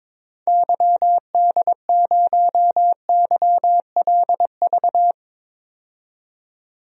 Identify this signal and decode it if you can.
Morse code, heard as a keyed tone, about 23 wpm, 700 Hz. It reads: YD0YLV
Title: DXCC-100 Call Signs - 6 Characters - 22wpm